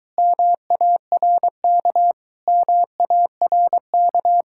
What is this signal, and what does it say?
Morse code, heard as a keyed tone, about 24 wpm, 700 Hz. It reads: MARK MARK